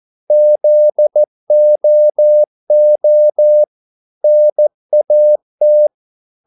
Morse code, heard as a keyed tone, about 14 wpm, 600 Hz. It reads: ZOO NAT
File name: Korte ord DK 14wpm